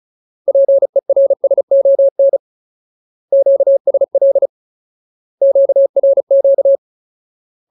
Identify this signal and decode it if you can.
Morse code, heard as a keyed tone, about 34 wpm, 550 Hz. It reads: PERSON QSL QRQ
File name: Ultimate Word List / 35WPM Ultimate